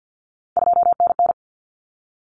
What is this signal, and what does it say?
Morse code, heard as a keyed tone, about 50 wpm, 700 Hz. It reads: 2NN
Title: Common Cut Numbers - 3-Digit 50wpm